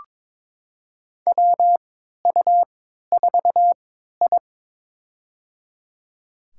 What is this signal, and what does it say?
Morse code, heard as a keyed tone, about 22 wpm, spaced as for 12 wpm, 700 Hz. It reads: WU4I
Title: USA Call Signs - 4 Characters - 3x Character Spacing - 22wpm